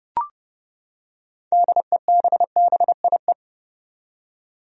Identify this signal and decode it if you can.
Morse code, heard as a keyed tone, about 30 wpm, 700 Hz. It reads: DEBBIE
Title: USA Common Names - 30wpm